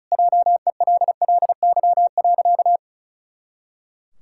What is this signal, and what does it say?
Morse code, heard as a keyed tone, about 36 wpm, 700 Hz. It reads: JELLY.
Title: Pangram Challenge v2 - 35wpm